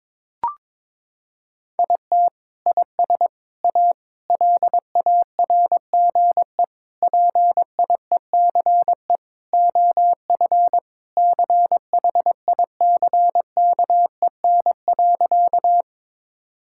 Morse code, heard as a keyed tone, about 22 wpm, 700 Hz. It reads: IT IS A LARGE PIECE OF CHICKEN.
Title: Sentences from Top 1000 Words - Review Part 5 - 22wpm